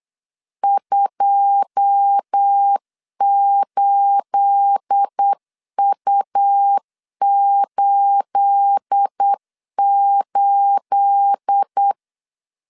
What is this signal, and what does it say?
Morse code, heard as a keyed tone, about 8 wpm, 800 Hz. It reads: 28U88
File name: LEZIONI / 9 28 8wpm 3512